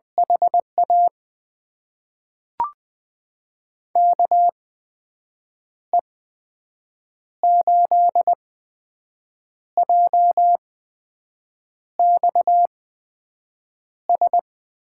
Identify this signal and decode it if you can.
Morse code, heard as a keyed tone, about 20 wpm, spaced as for 6 wpm, 700 Hz. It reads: HA KE8JXS
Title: USA Call Signs - 6 Characters - 8x Character Spacing - 20wpm